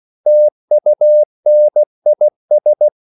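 Morse code, heard as a keyed tone, about 16 wpm, 600 Hz. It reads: TUNIS
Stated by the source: CW practice of national capitals at 16 words per minute, from ON6ZQ.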